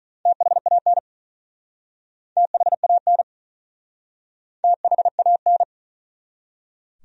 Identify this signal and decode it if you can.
Morse code, heard as a keyed tone, about 45 wpm, 700 Hz. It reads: THAN THAN THAN